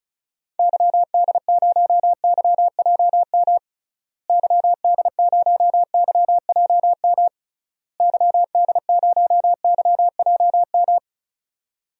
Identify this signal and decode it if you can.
Morse code, heard as a keyed tone, about 34 wpm, 700 Hz. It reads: YD0YJM YD0YJM YD0YJM
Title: DXCC-100 Call Signs - 6 Characters - Repeated 3x - 35wpm